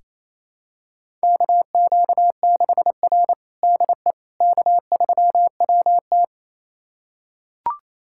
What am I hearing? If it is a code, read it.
Morse code, heard as a keyed tone, about 28 wpm, 700 Hz. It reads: KQ6R DE K3WT
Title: USA Callsign DE Callsign - 4 Characters - 28wpm